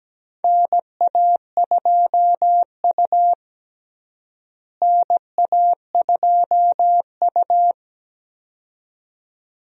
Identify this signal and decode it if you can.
Morse code, heard as a keyed tone, about 17 wpm, 700 Hz. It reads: NA2U NA2U
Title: USA Call Signs - 4 Characters - Repeated 2x - 17wpm